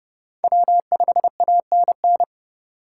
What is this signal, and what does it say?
Morse code, heard as a keyed tone, about 31 wpm, 700 Hz. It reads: W5ANN